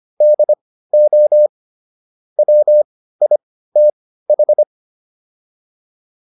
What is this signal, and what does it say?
Morse code, heard as a keyed tone, about 25 wpm, spaced as for 15 wpm, 600 Hz. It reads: DO WITH
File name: WD404 15